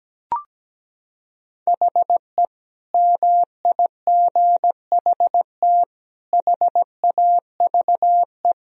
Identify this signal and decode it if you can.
Morse code, heard as a keyed tone, about 17 wpm, 700 Hz. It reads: HE MIGHT HAVE
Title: Sentences from Top 1000 Words - Review Part 3 - 17wpm